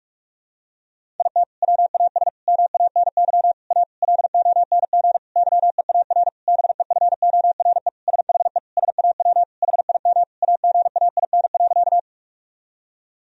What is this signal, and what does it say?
Morse code, heard as a keyed tone, about 46 wpm, 700 Hz. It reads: IT WAS MANY A LONG YEAR BEFORE SHE SAW HIM AGAIN.
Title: Sentences from Top 1000 Words - Review Part 1 - 45wpm